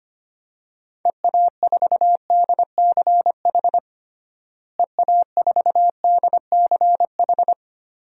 Morse code, heard as a keyed tone, about 25 wpm, 700 Hz. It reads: EA4DCH EA4DCH
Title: DXCC-100 Call Signs - 6 Characters - Repeated 2x - 25wpm